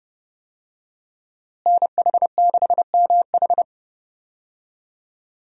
Morse code, heard as a keyed tone, about 32 wpm, 700 Hz. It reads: NH6MH
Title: USA Call Signs - 5 Characters - Repeated 2x - 30wpm